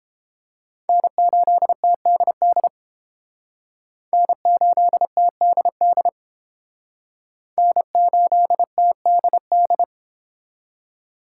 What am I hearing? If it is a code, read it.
Morse code, heard as a keyed tone, about 32 wpm, 700 Hz. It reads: N8TDD N8TDD N8TDD